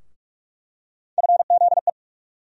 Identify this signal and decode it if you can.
Morse code, heard as a keyed tone, about 42 wpm, 700 Hz. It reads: FZE
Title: USA Call Sign Postfixes - 3 Letters 45wpm